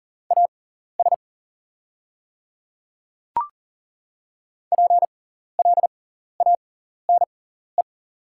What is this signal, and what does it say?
Morse code, heard as a keyed tone, about 40 wpm, spaced as for 14 wpm, 700 Hz. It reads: AS PLANE
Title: 6x Character Spacing - Top 500 Words - 40wpm